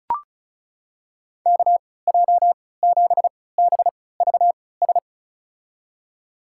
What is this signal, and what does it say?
Morse code, heard as a keyed tone, about 34 wpm, spaced as for 20 wpm, 700 Hz. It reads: KJ7BVS
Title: USA Call Signs - 3x Character Spacing - 35wpm